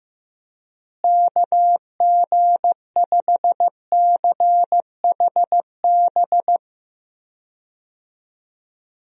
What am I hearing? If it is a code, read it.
Morse code, heard as a keyed tone, about 15 wpm, 700 Hz. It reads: KG5CHB
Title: USA Call Signs - 6 Characters - Repeated 2x - 15wpm